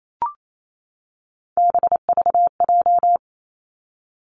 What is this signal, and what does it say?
Morse code, heard as a keyed tone, about 28 wpm, 700 Hz. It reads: BVJ
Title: Three Challenging Letter-Number Combinations 28wpm